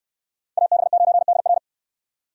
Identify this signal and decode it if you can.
Morse code, heard as a keyed tone, about 69 wpm, 700 Hz. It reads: AB9DR